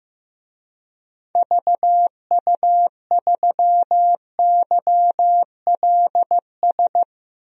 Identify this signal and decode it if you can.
Morse code, heard as a keyed tone, about 16 wpm, 700 Hz. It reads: VU3YLS
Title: DXCC-100 Call Signs - 6 Characters - 15wpm